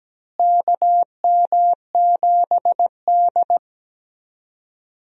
Morse code, heard as a keyed tone, about 17 wpm, 700 Hz. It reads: KM7D
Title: USA Call Signs - 4 Characters - Repeated 2x - 17wpm